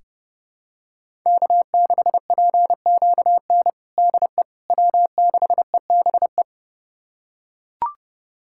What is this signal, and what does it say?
Morse code, heard as a keyed tone, about 30 wpm, 700 Hz. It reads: K6PQN DE W6EBE
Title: USA Callsign DE Callsign - 5 Characters - 30wpm